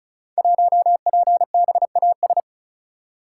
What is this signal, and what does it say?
Morse code, heard as a keyed tone, about 36 wpm, 700 Hz. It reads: 1PBAS
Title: Code Groups - 5 Letter:Numbers - 35wpm